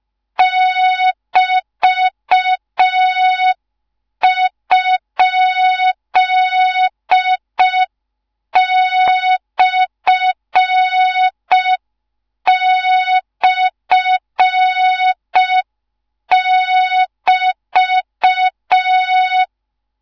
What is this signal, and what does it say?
Morse code, heard as a keyed tone, about 5 wpm, 3700 Hz. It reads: <BT>?//<BT>
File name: Morze-HA2MN / hang / 13IRASJELEK1 HA2MN